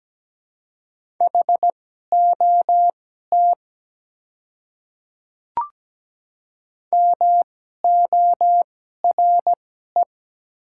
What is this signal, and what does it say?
Morse code, heard as a keyed tone, about 17 wpm, spaced as for 12 wpm, 700 Hz. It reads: HOT MORE
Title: 2x Character Spacing - Top 100 Words - 17wpm